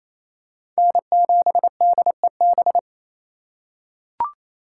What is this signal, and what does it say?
Morse code, heard as a keyed tone, about 28 wpm, 700 Hz. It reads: N7DEB
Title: USA Call Signs - 5 Characters - 4x Character Spacing - 28wpm